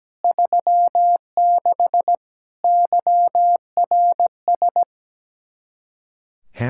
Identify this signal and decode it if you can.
Morse code, heard as a keyed tone, about 17 wpm, 700 Hz. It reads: 36 YRS
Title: Practice QSO - Exchange 3 - More Info - 17wpm